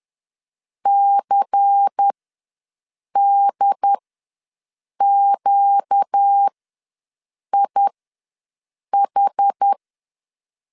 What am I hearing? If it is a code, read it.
Morse code, heard as a keyed tone, about 10 wpm, spaced as for 6 wpm, 800 Hz. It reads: CDQIH
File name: LEZIONI / 5 LBVC / 5 LBVC ws 10wpm 3733